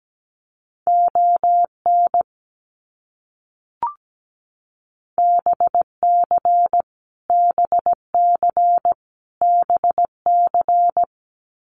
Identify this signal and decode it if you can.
Morse code, heard as a keyed tone, about 17 wpm, 700 Hz. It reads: ON BC BC BC
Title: Canadian Provinces and Territories - Repeated 3x - 17wpm